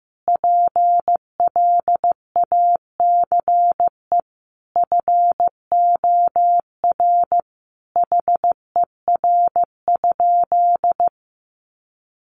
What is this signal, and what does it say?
Morse code, heard as a keyed tone, about 15 wpm, 700 Hz. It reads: PLACE FOR HER?
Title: Sentences from Top 800 Words 15wpm